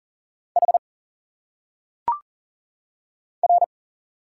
Morse code, heard as a keyed tone, about 40 wpm, 700 Hz. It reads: H R
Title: Single Letters - Phonetic Pronunciation 40wpm